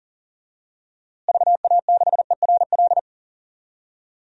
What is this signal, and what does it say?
Morse code, heard as a keyed tone, about 37 wpm, 700 Hz. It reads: VA6ERL
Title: DXCC-100 Call Signs - 6 Characters - Repeated 2x - 40wpm